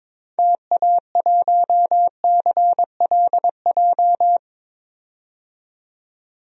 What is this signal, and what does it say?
Morse code, heard as a keyed tone, about 22 wpm, 700 Hz. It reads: TA1CLJ
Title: DXCC-100 Call Signs - 6 Characters - 22wpm